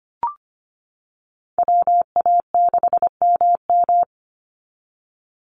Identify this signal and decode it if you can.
Morse code, heard as a keyed tone, about 25 wpm, 690 Hz. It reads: WA6MM
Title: NA SOTA Activator Callsigns 25wpm